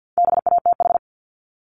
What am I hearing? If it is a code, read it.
Morse code, heard as a keyed tone, about 50 wpm, 700 Hz. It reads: BATH
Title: Multi-letter Phonemes 50wpm